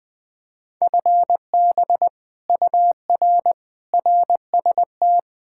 Morse code, heard as a keyed tone, about 20 wpm, 700 Hz. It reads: FB UR RST
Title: Speed-Racing Common QSO Phrases - 17wpm